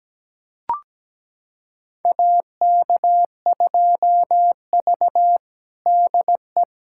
Morse code, heard as a keyed tone, about 17 wpm, 700 Hz. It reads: AK2V DE
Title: USA Callsign DE Callsign - 4 Characters - 17wpm